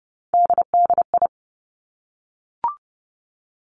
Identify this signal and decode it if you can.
Morse code, heard as a keyed tone, about 30 wpm, 700 Hz. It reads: DDI
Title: Three Challenging Letter-Number Combinations 30wpm